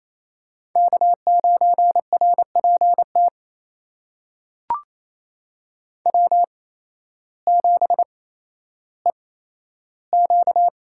Morse code, heard as a keyed tone, about 28 wpm, spaced as for 13 wpm, 700 Hz. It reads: K9RPT W7EQ
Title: USA Call Signs - 8x Character Spacing - 28wpm